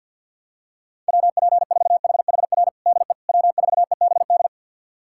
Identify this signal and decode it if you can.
Morse code, heard as a keyed tone, about 47 wpm, 700 Hz. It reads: WP4HHR DE W4EBD
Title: USA Callsign DE Callsign 50wpm